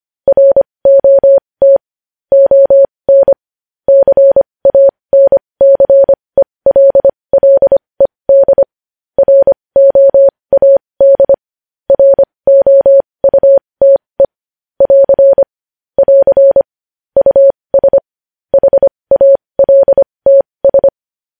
Morse code, heard as a keyed tone, about 25 wpm, spaced as for 20 wpm, 550 Hz. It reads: ROT ON CANCELLED ROAD ROUTE <AR> <AR> US HALTS
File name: podcast / News headlines-1x-25x20